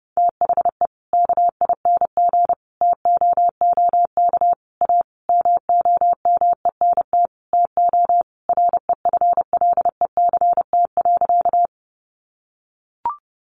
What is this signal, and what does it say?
Morse code, heard as a keyed tone, about 30 wpm, 700 Hz. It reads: THE KING TOOK A MOMENT TO REFLECT.
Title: Sentences from Top 1500 Words 30wpm